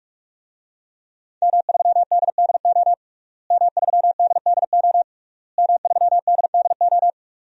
Morse code, heard as a keyed tone, about 41 wpm, 700 Hz. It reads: M3DDO M3DDO M3DDO
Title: DXCC-100 Call Signs - 5 Characters - Repeated 3x - 45wpm